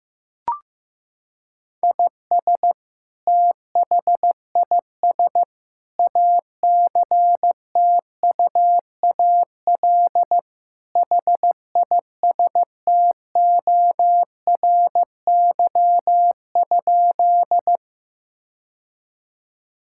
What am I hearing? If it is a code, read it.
Morse code, heard as a keyed tone, about 16 wpm, 700 Hz. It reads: IS THIS ACTUAL HISTORY?
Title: Sentences from Top 1000 Words - Review Part 5 - 15wpm